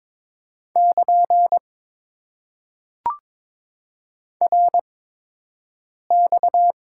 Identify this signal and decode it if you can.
Morse code, heard as a keyed tone, about 22 wpm, spaced as for 6 wpm, 700 Hz. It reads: <KN> RX
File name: Top 90 QSO Elements - 8x Character Spacing - 22wpm